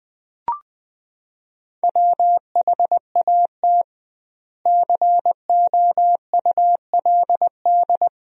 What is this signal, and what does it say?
Morse code, heard as a keyed tone, about 20 wpm, 700 Hz. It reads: WHAT COULD
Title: 2x Word-Spacing - Sentences from Top 400 Words - 20wpm